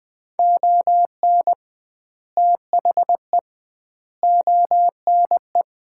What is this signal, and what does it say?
Morse code, heard as a keyed tone, about 21 wpm, 700 Hz. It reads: ON THE ONE
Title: 2x Word-Spacing - Sets of 4 Words - 20wpm